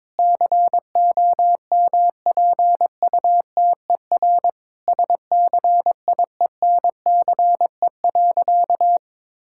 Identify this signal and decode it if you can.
Morse code, heard as a keyed tone, about 22 wpm, 700 Hz. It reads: COMPUTER SCIENCE.
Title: Sentences from Top 1000 Words - Review Part 6 - 22wpm